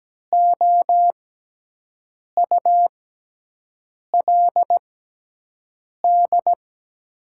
Morse code, heard as a keyed tone, about 17 wpm, spaced as for 6 wpm, 700 Hz. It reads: OULD
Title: 6x Character Spacing - Top 300 Words - 17wpm